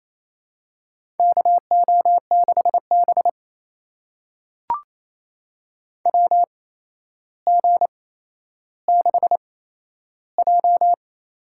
Morse code, heard as a keyed tone, about 28 wpm, spaced as for 12 wpm, 700 Hz. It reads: KO6B WG6J
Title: USA Call Signs - 4 Characters - 8x Character Spacing - 28wpm